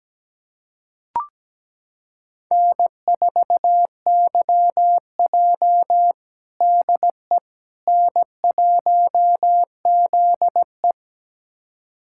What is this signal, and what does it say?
Morse code, heard as a keyed tone, about 17 wpm, 700 Hz. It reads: N4YJ DE N1ZE
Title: USA Callsign DE Callsign - 4 Characters - 17wpm